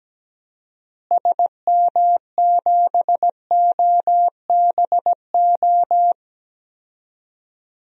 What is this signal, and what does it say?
Morse code, heard as a keyed tone, about 17 wpm, 700 Hz. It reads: SM7OBO
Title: DXCC-100 Call Signs - 6 Characters - Repeated 3x - 17wpm